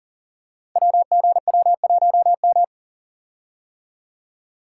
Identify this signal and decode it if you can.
Morse code, heard as a keyed tone, about 38 wpm, 690 Hz. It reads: WGW1M
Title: Code Groups - 5 Letter:Numbers - 40wpm